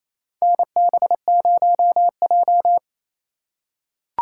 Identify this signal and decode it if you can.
Morse code, heard as a keyed tone, about 28 wpm, 700 Hz. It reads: NB0J
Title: USA Call Signs - 3x Character Spacing - 28wpm